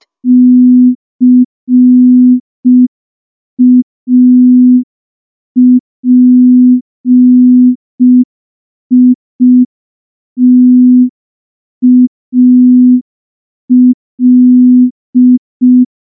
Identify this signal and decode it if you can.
Morse code, heard as a keyed tone, about 5 wpm, 250 Hz. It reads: CAPITAL
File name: morse / morse-training-05WPM